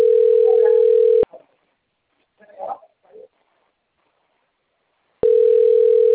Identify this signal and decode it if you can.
Ringback tone (North America), 440 + 480 Hz.
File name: out- - - - - .73975